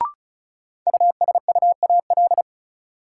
Morse code, heard as a keyed tone, about 36 wpm, 700 Hz. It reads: USUAL
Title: Sentences from Top 400 Words - New Word Drill - 35wpm